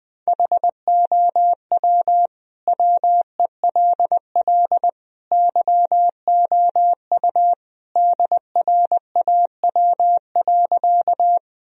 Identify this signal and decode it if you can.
Morse code, heard as a keyed tone, about 20 wpm, 700 Hz. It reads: HOW WELL YOU DRAW.
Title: Sentences from Top 1000 Words - Review Part 1 - 20wpm